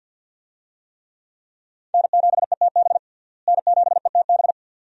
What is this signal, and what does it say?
Morse code, heard as a keyed tone, about 52 wpm, 700 Hz. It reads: N7ETB N7ETB
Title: USA Call Signs - 5 Characters - Repeated 2x - 50wpm